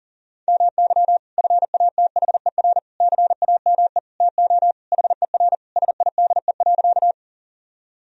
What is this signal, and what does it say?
Morse code, heard as a keyed tone, about 40 wpm, 700 Hz. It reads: MY FATHER CAME TO HER SIDE.
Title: Sentences from Top 1000 Words - Review Part 1 - 40wpm